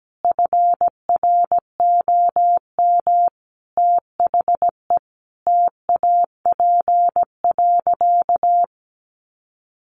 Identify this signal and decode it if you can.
Morse code, heard as a keyed tone, about 17 wpm, 700 Hz. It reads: FROM THE TAP.
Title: Sentences from Top 1400 Words 17wpm